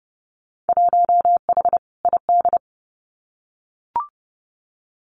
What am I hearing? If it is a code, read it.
Morse code, heard as a keyed tone, about 30 wpm, 700 Hz. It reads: 1H ID
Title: Winter Field Day Exchanges 30wpm